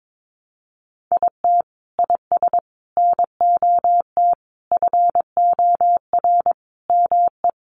Morse code, heard as a keyed tone, about 22 wpm, 700 Hz. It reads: IT IS NOT FOR ME
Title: Sentences from Top 600 Words 22wpm